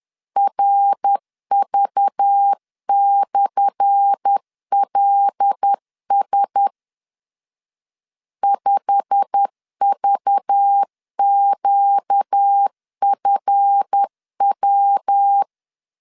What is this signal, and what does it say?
Morse code, heard as a keyed tone, about 10 wpm, 800 Hz. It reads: RV/LS 5VQFW
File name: LEZIONI / 10 SIGNS / 10C DOT 10wpm 2940